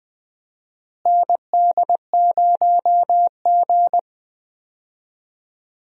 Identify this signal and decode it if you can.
Morse code, heard as a keyed tone, about 20 wpm, 700 Hz. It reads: ND0G
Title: USA Call Signs - 4 Characters - Repeated 2x - 20wpm